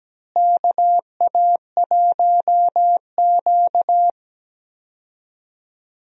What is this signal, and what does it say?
Morse code, heard as a keyed tone, about 17 wpm, 700 Hz. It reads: KA1Q
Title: USA Call Signs - 4 Characters - Repeated 2x - 17wpm